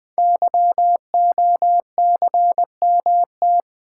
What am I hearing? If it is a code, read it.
Morse code, heard as a keyed tone, about 20 wpm, 700 Hz. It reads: YOCMT
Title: Code Groups - 5 Letters - 20wpm